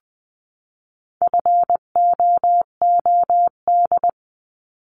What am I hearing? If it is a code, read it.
Morse code, heard as a keyed tone, about 20 wpm, 700 Hz. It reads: FOOD
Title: 259 Lesson 42d - Words 18-20wpm